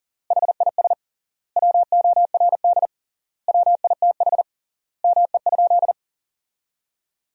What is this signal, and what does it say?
Morse code, heard as a keyed tone, about 40 wpm, 700 Hz. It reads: HIS WORD WITH ME?
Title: 3x Word-Spacing - Sentences from Top 300 Words - 40wpm